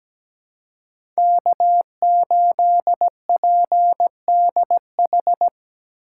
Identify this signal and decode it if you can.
Morse code, heard as a keyed tone, about 17 wpm, 700 Hz. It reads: K8PDH
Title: USA Call Signs - 3x Character Spacing - 17wpm